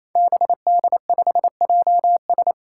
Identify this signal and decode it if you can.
Morse code, heard as a keyed tone, about 28 wpm, 700 Hz. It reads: BD5JS
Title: Five Challenging Letter-Number Combinations 28wpm